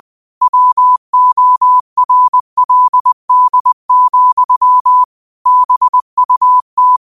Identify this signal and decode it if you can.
Morse code, heard as a keyed tone, about 20 wpm, 1000 Hz. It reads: WORLD, BUT